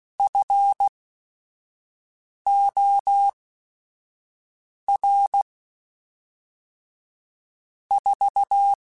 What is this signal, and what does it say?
Morse code, heard as a keyed tone, about 16 wpm, spaced as for 5 wpm, 800 Hz. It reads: FOR 4
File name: Technical Subjects / Technical Subjects-mp3s / AND Gate @5